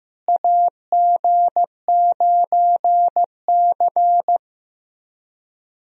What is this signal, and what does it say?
Morse code, heard as a keyed tone, about 15 wpm, 700 Hz. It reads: AG9C
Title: USA Call Signs - 4 Characters - Repeated 2x - 15wpm